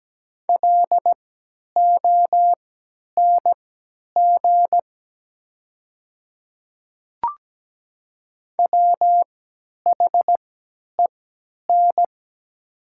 Morse code, heard as a keyed tone, about 17 wpm, spaced as for 10 wpm, 700 Hz. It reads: LONG WHEN
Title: 3x Character Spacing - Top 200 Words - 17wpm